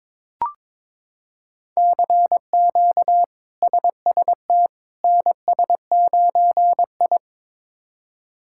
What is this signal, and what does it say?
Morse code, heard as a keyed tone, about 22 wpm, 700 Hz. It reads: CQ SST NS9I
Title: K1USN SST QSO Practice - 22wpm